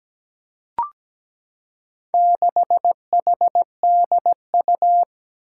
Morse code, heard as a keyed tone, about 17 wpm, 700 Hz. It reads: 6HDU
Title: Four Challenging Letter-Number Combinations 17wpm